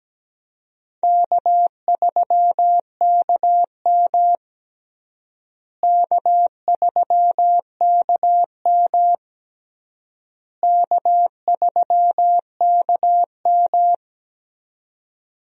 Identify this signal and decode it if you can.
Morse code, heard as a keyed tone, about 17 wpm, 700 Hz. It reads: K3KM K3KM K3KM
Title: USA Call Signs - 4 Characters - Repeated 3x - 17wpm